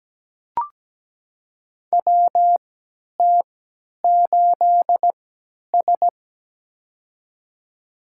Morse code, heard as a keyed tone, about 17 wpm, spaced as for 10 wpm, 700 Hz. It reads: WT8S
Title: USA Call Signs - 4 Characters - 3x Character Spacing - 17wpm